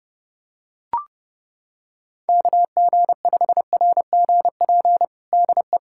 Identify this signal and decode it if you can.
Morse code, heard as a keyed tone, about 32 wpm, 700 Hz. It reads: KG5RGP DE
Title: USA Callsign DE Callsign 30wpm